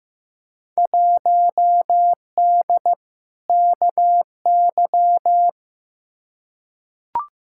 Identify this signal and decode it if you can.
Morse code, heard as a keyed tone, about 15 wpm, 700 Hz. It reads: 1D KY
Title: Field Day Exchanges - 15wpm